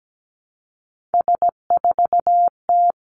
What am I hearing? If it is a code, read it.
Morse code, heard as a keyed tone, about 17 wpm, 700 Hz. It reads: S4T
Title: Three Challenging Letter-Number Combinations 17wpm